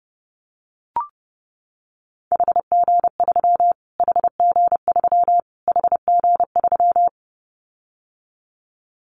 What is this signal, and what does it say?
Morse code, heard as a keyed tone, about 30 wpm, 700 Hz. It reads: HG3 HG3 HG3
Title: Callsign Prefixes for Top 25 DXCC - Repeated 3x - 30wpm